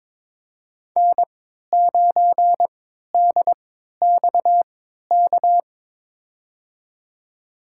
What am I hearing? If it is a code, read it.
Morse code, heard as a keyed tone, about 22 wpm, spaced as for 12 wpm, 700 Hz. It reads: N9DXK
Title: USA Call Signs - 5 Characters - 3x Character Spacing - 22wpm